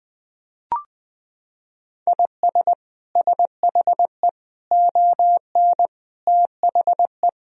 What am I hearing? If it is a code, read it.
Morse code, heard as a keyed tone, about 20 wpm, 700 Hz. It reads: IS SHE ON THE
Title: Sentences from Top 1000 Words - Review Part 2 - 20wpm